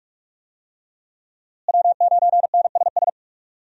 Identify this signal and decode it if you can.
Morse code, heard as a keyed tone, about 46 wpm, 700 Hz. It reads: W9NSS
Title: USA Call Signs - 5 Characters - Repeated 3x - 45wpm